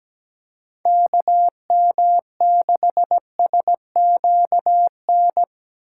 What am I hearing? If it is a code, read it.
Morse code, heard as a keyed tone, about 17 wpm, 700 Hz. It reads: KM6SQN
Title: USA Call Signs - 2x Character Spacing - 17wpm